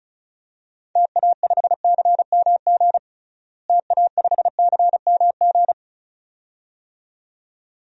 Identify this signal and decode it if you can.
Morse code, heard as a keyed tone, about 36 wpm, 700 Hz. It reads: TA5CMG TA5CMG
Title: DXCC-100 Call Signs - 6 Characters - Repeated 2x - 35wpm